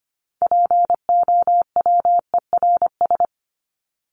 Morse code, heard as a keyed tone, about 25 wpm, 700 Hz. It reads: POWERS
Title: ES:S Suffix - Top 100 Words with Base Words 25wpm